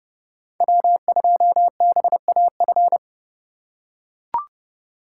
Morse code, heard as a keyed tone, about 30 wpm, 700 Hz. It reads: W2BAF
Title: USA Call Signs - 5 Characters - 3x Character Spacing - 30wpm